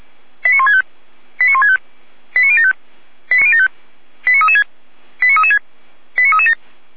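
Five-tone selective call (ZVEI), heard as quick runs of five tones, tones of about 70 ms. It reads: 89165 89165 89075 89336 89336 89337
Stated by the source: ZVEI five-tone call received by SDR, Feuerwehr Orlamünde.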